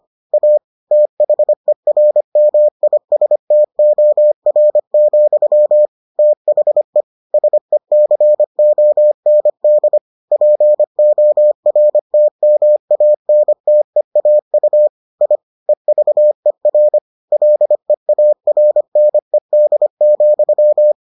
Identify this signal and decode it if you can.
Morse code, heard as a keyed tone, about 25 wpm, 600 Hz. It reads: A THERMISTOR, THE SECOND PORTMANTEAU I EVER LEARNED,